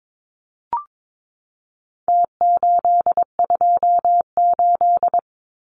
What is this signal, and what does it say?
Morse code, heard as a keyed tone, about 22 wpm, 700 Hz. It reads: T828
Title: Numbers - Contest Style - 4-Digit 22wpm